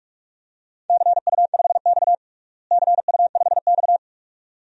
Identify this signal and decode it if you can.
Morse code, heard as a keyed tone, about 43 wpm, 700 Hz. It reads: CU5X CU5X
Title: DXCC-100 Call Signs - 4 Characters - Repeated 2x - 45wpm